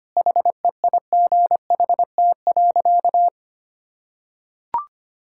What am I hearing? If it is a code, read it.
Morse code, heard as a keyed tone, about 25 wpm, 700 Hz. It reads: HEIGHT.
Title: Sentences from Top 1600 Words 25wpm